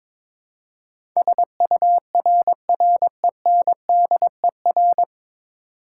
Morse code, heard as a keyed tone, about 22 wpm, 700 Hz. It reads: SURRENDER
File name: 9 Letter Words - First 3 Letters Given - 22wpm